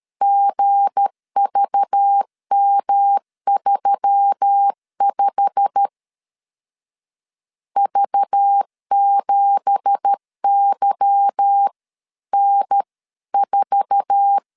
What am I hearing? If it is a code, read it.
Morse code, heard as a keyed tone, about 13 wpm, 800 Hz. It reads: GVM35 V7YN4
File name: CW / DET / 1 9 1500C S B 12wpm 3819